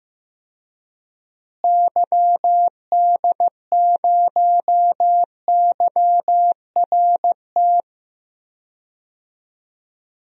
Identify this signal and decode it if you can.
Morse code, heard as a keyed tone, about 15 wpm, 700 Hz. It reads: YD0YRT
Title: DXCC-100 Call Signs - 6 Characters - Repeated 2x - 15wpm